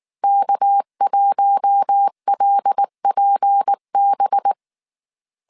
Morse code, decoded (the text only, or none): K1LPB